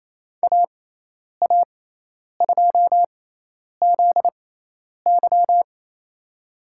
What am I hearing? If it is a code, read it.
Morse code, heard as a keyed tone, about 28 wpm, spaced as for 10 wpm, 700 Hz. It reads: AA2ZY
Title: USA Call Signs - 5 Characters - 6x Character Spacing - 28wpm